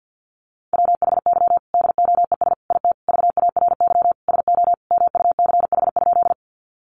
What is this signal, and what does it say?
Morse code, heard as a keyed tone, about 50 wpm, 700 Hz. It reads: WHY DOES IT VARY SO MUCH?
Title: Sentences from Top 1200 Words 50wpm